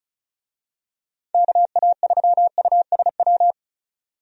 Morse code, heard as a keyed tone, about 35 wpm, 700 Hz. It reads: KA3USW
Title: USA Call Signs - 6 Characters - 35wpm